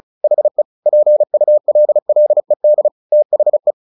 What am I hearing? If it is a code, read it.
Morse code, heard as a keyed tone, about 34 wpm, 600 Hz. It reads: HE PULLED THE
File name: the old man and the sea 2 35wpm